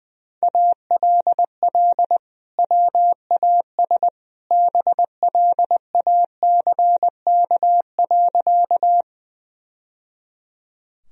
Morse code, heard as a keyed tone, about 21 wpm, 700 Hz. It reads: ALL WAS BLACK.
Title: Sentences from Top 1000 Words - Review Part 2 - 20wpm